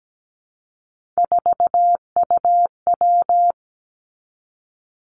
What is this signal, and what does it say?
Morse code, heard as a keyed tone, about 17 wpm, 700 Hz. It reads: 4UW
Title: Three Challenging Letter-Number Combinations 17wpm